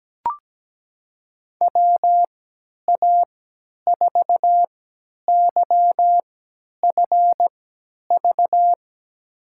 Morse code, heard as a keyed tone, about 17 wpm, spaced as for 10 wpm, 700 Hz. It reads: WA4YFV
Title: USA Call Signs - 6 Characters - 3x Character Spacing - 17wpm